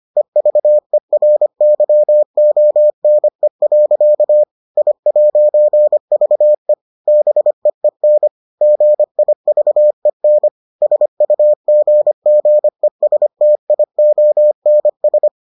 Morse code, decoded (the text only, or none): EVERYONE. I'VE BEEN GIVEN SUGGESTIONS